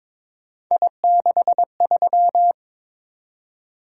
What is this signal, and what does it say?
Morse code, heard as a keyed tone, about 22 wpm, 700 Hz. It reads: I63
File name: practice / 22wpm / Three Letter-Number Combinations - 22wpm